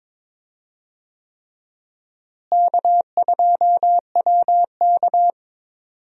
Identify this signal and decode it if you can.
Morse code, heard as a keyed tone, about 21 wpm, 700 Hz. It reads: K2WK
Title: USA Call Signs - 4 Characters - Repeated 3x - 22wpm